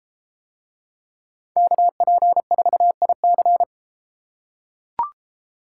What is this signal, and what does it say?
Morse code, heard as a keyed tone, about 33 wpm, 700 Hz. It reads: KP4IC